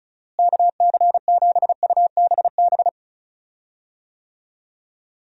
Morse code, heard as a keyed tone, about 36 wpm, 700 Hz. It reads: KC7UBB
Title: USA Call Signs - 6 Characters - Repeated 2x - 35wpm